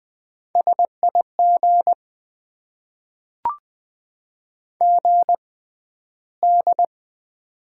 Morse code, heard as a keyed tone, about 20 wpm, spaced as for 12 wpm, 700 Hz. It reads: SIG GD
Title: Top 90 QSO Elements - 6x Character Spacing - 20wpm